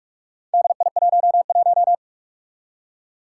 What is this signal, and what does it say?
Morse code, heard as a keyed tone, about 44 wpm, 700 Hz. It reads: DI11
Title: Four Challenging Letter-Number Combinations 45wpm